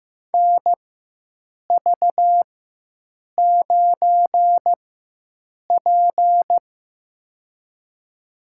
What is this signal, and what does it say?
Morse code, heard as a keyed tone, about 15 wpm, spaced as for 7 wpm, 700 Hz. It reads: NV9P
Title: USA Call Signs - 4x Character Spacing - 15wpm